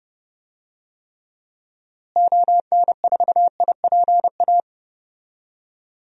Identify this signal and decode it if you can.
Morse code, heard as a keyed tone, about 30 wpm, 700 Hz. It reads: ON4IPA
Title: DXCC-100 Call Signs - 6 Characters - Repeated 3x - 30wpm